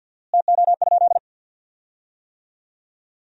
Morse code, heard as a keyed tone, about 50 wpm, 700 Hz. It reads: TO?
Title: 4x Word-Spacing - Sentences from Top 100 Words - 50wpm